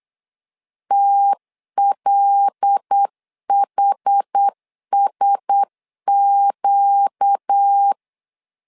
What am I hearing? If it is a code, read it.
Morse code, heard as a keyed tone, about 8 wpm, 800 Hz. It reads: TLHSQ
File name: LEZIONI / 5 LBVC / 5 LBV 8wpm 2837